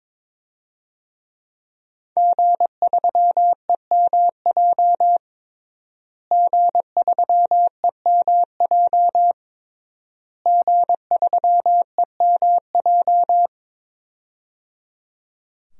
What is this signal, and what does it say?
Morse code, heard as a keyed tone, about 22 wpm, 700 Hz. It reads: G3EMJ G3EMJ G3EMJ
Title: DXCC-100 Call Signs - 5 Characters - Repeated 3x - 22wpm